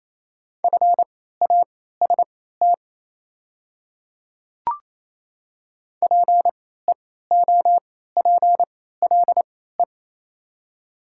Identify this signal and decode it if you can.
Morse code, heard as a keyed tone, about 28 wpm, spaced as for 16 wpm, 700 Hz. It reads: FAST PEOPLE
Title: 3x Character Spacing - Top 500 Words - 28wpm